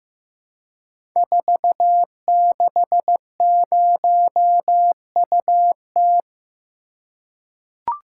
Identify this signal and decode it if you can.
Morse code, heard as a keyed tone, about 15 wpm, 700 Hz. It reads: 460UT
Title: Code Groups - 5 Letter:Numbers - 15wpm